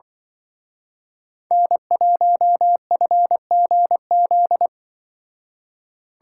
Morse code, heard as a keyed tone, about 23 wpm, 700 Hz. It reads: N1FGZ
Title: Speed Racing - USA Call Signs 20wpm